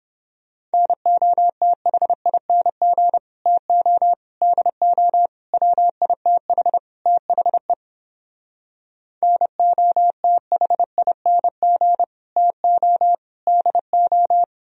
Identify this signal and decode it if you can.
Morse code, heard as a keyed tone, about 29 wpm, 700 Hz. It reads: NOTHING TO DO WITH THE NOTHING TO DO